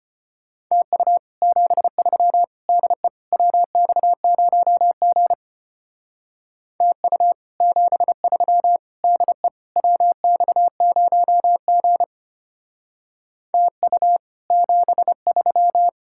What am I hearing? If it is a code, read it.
Morse code, heard as a keyed tone, about 29 wpm, 700 Hz. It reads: TU 73 DE WX0G TU 73 DE WX0G TU 73
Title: Speed-Racing Common QSO Phrases - 25wpm